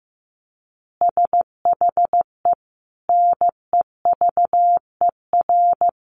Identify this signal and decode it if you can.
Morse code, heard as a keyed tone, about 15 wpm, 700 Hz. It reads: SHE NEVER
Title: Easy Phrases 15wpm